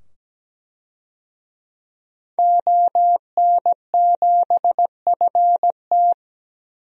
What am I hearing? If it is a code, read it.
Morse code, heard as a keyed tone, about 17 wpm, 700 Hz. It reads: ON7FT
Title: DXCC-100 Call Signs - 5 Characters - 17wpm